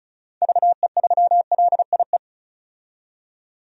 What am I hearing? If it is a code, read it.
Morse code, heard as a keyed tone, about 34 wpm, 700 Hz. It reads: VE3LIE
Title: DXCC-100 Call Signs - 6 Characters - 35wpm